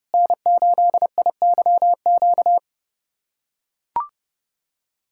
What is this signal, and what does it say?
Morse code, heard as a keyed tone, about 30 wpm, 700 Hz. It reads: N8IYQ
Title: USA Callsign DE Callsign 30wpm